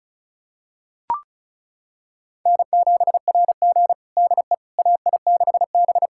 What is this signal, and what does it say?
Morse code, heard as a keyed tone, about 36 wpm, 700 Hz. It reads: N7RG DE AI6B
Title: USA Callsign DE Callsign - 4 Characters - 35wpm